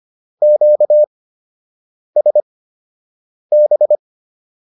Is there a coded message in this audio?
Yes: Morse code from 0.4 to 4.0 s, heard as a keyed tone, about 25 wpm, spaced as for 7 wpm, 600 Hz. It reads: QSB